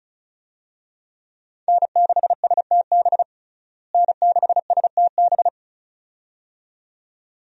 Morse code, heard as a keyed tone, about 36 wpm, 700 Hz. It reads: N6STB N6STB
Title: USA Call Signs - 5 Characters - Repeated 2x - 35wpm